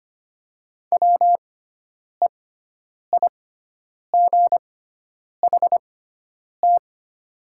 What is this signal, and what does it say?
Morse code, heard as a keyed tone, about 25 wpm, spaced as for 9 wpm, 700 Hz. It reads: WEIGHT
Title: 6x Character Spacing - Top 500 Words - 25wpm